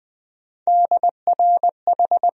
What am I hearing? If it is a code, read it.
Morse code, heard as a keyed tone, about 20 wpm, 700 Hz. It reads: DRH
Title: USA Call Sign Postfixes - 3 Letters 20wpm